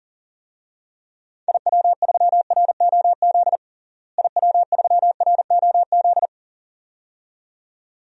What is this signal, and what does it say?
Morse code, heard as a keyed tone, about 40 wpm, 700 Hz. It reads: IW3ROZ IW3ROZ
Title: DXCC-100 Call Signs - 6 Characters - Repeated 2x - 40wpm